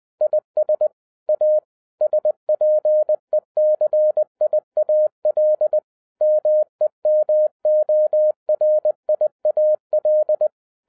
Morse code, heard as a keyed tone, about 20 wpm, 610 Hz. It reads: IS A SPECIAL MEMORIAL